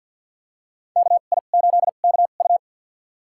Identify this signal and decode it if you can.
Morse code, heard as a keyed tone, about 52 wpm, spaced as for 36 wpm, 700 Hz. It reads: KI8KU
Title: USA Call Signs - 2x Character Spacing - 50wpm